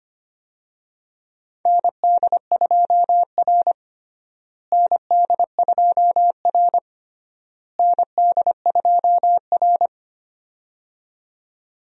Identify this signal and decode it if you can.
Morse code, heard as a keyed tone, about 25 wpm, 700 Hz. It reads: ND2R ND2R ND2R
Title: USA Call Signs - 4 Characters - Repeated 3x - 25wpm